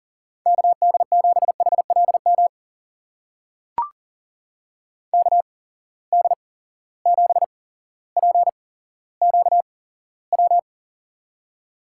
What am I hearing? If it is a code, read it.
Morse code, heard as a keyed tone, about 40 wpm, spaced as for 17 wpm, 700 Hz. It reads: KD7HLM KD7PQW
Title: USA Call Signs - 6 Characters - 8x Character Spacing - 40wpm